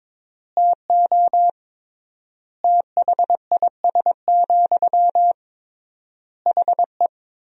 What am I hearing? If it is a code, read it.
Morse code, heard as a keyed tone, about 22 wpm, 700 Hz. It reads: TO THIS, HE
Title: 3x Word-Spacing - Sentences from Top 400 Words - 22wpm